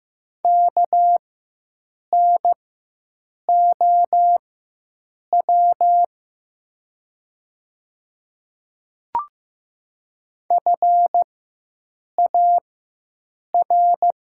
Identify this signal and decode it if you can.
Morse code, heard as a keyed tone, about 15 wpm, spaced as for 7 wpm, 700 Hz. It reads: KNOW FAR
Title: 4x Character Spacing - Top 300 Words - 15wpm